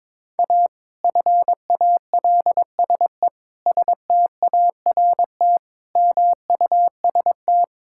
Morse code, heard as a keyed tone, about 22 wpm, 700 Hz. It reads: A FALSE START MUST
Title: Sentences from Top 1600 Words 22wpm